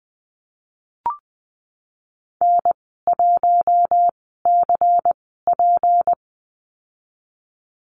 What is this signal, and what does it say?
Morse code, heard as a keyed tone, about 20 wpm, spaced as for 15 wpm, 700 Hz. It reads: N1CP
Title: 154 Lesson 25g - Letter p - Call Signs 20wpm